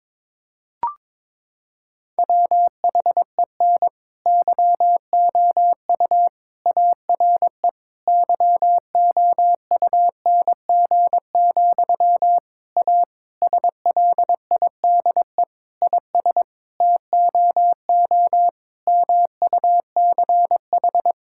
Morse code, heard as a keyed tone, about 22 wpm, 700 Hz. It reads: WHEN YOU ARE YOUNG, A SLIDE IS TOO MUCH
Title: Sentences from Top 1600 Words 22wpm